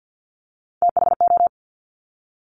Morse code, heard as a keyed tone, about 50 wpm, 700 Hz. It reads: THO
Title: Top Three Letter Combinations in Words 50wpm